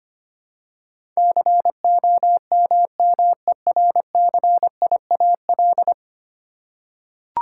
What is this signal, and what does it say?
Morse code, heard as a keyed tone, about 25 wpm, 700 Hz. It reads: COMMERCIAL
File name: 10 Letter Words - First 3 Letters Given - 25wpm